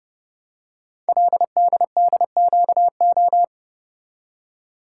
Morse code, heard as a keyed tone, about 28 wpm, 700 Hz. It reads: LDDQO
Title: Code Groups - 5 Letters - 30wpm